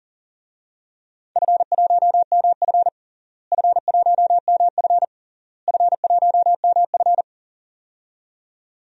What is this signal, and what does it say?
Morse code, heard as a keyed tone, about 40 wpm, 700 Hz. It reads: F1MF F1MF F1MF
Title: DXCC-100 Call Signs - 4 Characters - Repeated 3x - 40wpm